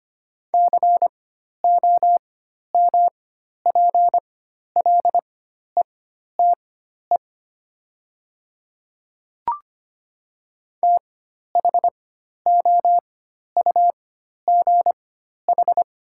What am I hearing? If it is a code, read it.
Morse code, heard as a keyed tone, about 25 wpm, spaced as for 12 wpm, 700 Hz. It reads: COMPLETE THOUGH
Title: 4x Character Spacing - Top 500 Words - 25wpm